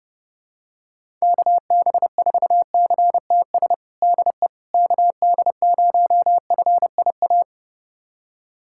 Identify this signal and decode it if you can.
Morse code, heard as a keyed tone, about 32 wpm, 700 Hz. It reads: KB4CTS DE KD0FIA
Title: USA Callsign DE Callsign 30wpm